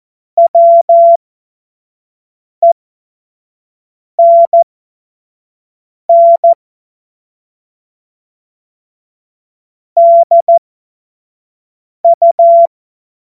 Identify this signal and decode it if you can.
Morse code, heard as a keyed tone, about 14 wpm, spaced as for 5 wpm, 680 Hz. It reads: WENN DU
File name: Morsekurs / Witze / Witz 09. Dezember 2025 25 F70